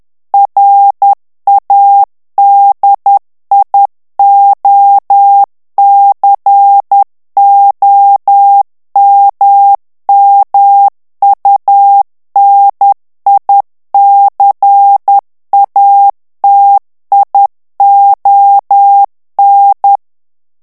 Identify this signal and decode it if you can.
Morse code, heard as a keyed tone, about 10 wpm, 800 Hz. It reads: RADIOCOMMUNICATION
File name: ANGLAIS texte clair 3 vitesse 10 mots minute